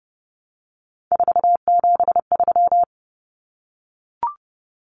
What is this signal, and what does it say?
Morse code, heard as a keyed tone, about 30 wpm, 700 Hz. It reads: 473
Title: Numbers - 3-Digit 30wpm